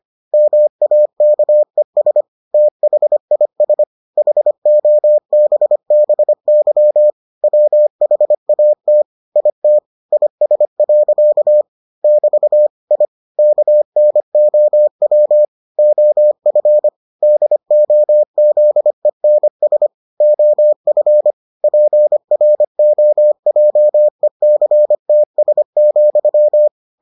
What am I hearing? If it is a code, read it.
Morse code, heard as a keyed tone, about 25 wpm, 600 Hz. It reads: MAKES THIS HOBBY WHAT IT IS. <BT> I KNOW OF DOZENS OF PROJECTS,